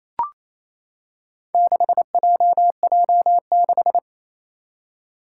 Morse code, heard as a keyed tone, about 28 wpm, 700 Hz. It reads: 6JJ6
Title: Four Challenging Letter-Number Combinations 28wpm